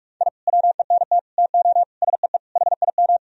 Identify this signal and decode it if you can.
Morse code, heard as a keyed tone, about 47 wpm, 700 Hz. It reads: I WENT TO SEE HIM
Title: Sentences from Top 1000 Words - Review Part 1 - 45wpm